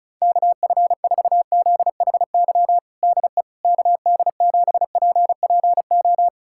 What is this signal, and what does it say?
Morse code, heard as a keyed tone, about 34 wpm, 700 Hz. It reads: KF4ZHY DE KD7PPO
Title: USA Callsign DE Callsign 35wpm